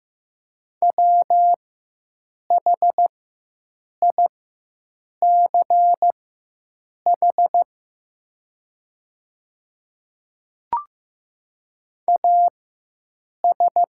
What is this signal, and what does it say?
Morse code, heard as a keyed tone, about 15 wpm, spaced as for 7 wpm, 700 Hz. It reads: WHICH AS
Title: 4x Character Spacing - Top 100 Words - 15wpm